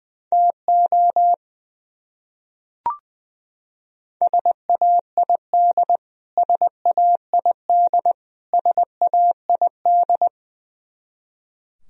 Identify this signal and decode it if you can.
Morse code, heard as a keyed tone, about 21 wpm, 700 Hz. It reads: TO SAID SAID SAID
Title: Top 100 Words Repeated 20wpm